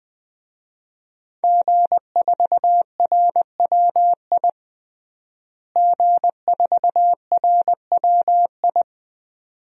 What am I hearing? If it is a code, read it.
Morse code, heard as a keyed tone, about 20 wpm, 700 Hz. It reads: G4RWI G4RWI
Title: DXCC-100 Call Signs - 5 Characters - Repeated 2x - 20wpm